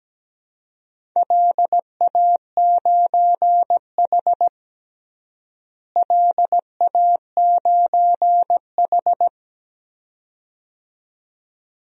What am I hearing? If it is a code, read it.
Morse code, heard as a keyed tone, about 17 wpm, 700 Hz. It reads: LA9H LA9H
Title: DXCC-100 Call Signs - 4 Characters - Repeated 2x - 17wpm